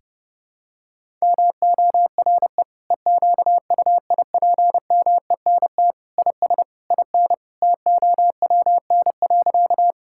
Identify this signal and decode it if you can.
Morse code, heard as a keyed tone, about 30 wpm, 700 Hz. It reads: MORE EQUIPMENT IS IN TOWN.
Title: Sentences from Top 1000 Words - Review Part 5 - 30wpm